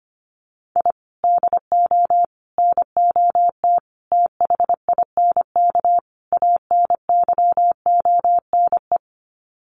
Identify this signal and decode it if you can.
Morse code, heard as a keyed tone, about 25 wpm, 700 Hz. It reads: I DO NOT THINK ANYONE
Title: Sets of 5 Words Encore 25wpm